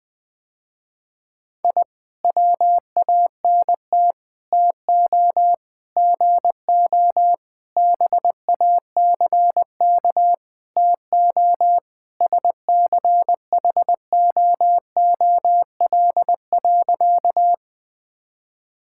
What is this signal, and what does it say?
Morse code, heard as a keyed tone, about 19 wpm, 700 Hz. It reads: I WANT TO GO BACK TO SCHOOL.
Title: Sentences with 4-Word Phrases - Spoken Beginning - 20wpm